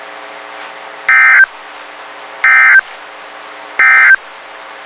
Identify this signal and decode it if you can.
Emergency Alert System (SAME) tones, heard as harsh buzzing bursts. It is heard from 1.0 to 4.2 s.
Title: WXK91 NOAA Weather Radio